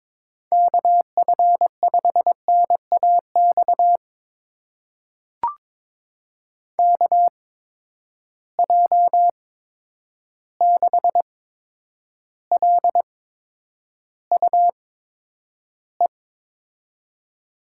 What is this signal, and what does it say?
Morse code, heard as a keyed tone, about 22 wpm, spaced as for 9 wpm, 700 Hz. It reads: KF5NAX KJ6LUE
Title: USA Call Signs - 6 Characters - 8x Character Spacing - 22wpm